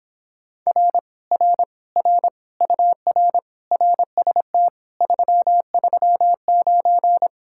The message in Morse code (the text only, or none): R R R UR RST 339